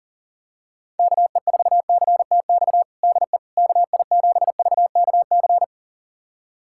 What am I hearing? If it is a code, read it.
Morse code, heard as a keyed tone, about 40 wpm, 700 Hz. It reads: KE4CTX DE KI7VKC
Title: USA Callsign DE Callsign - 6 Characters - 40wpm